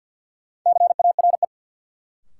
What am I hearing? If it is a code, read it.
Morse code, heard as a keyed tone, about 53 wpm, 700 Hz. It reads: CARE
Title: Sets of 3 Words - New Word Drill - 50wpm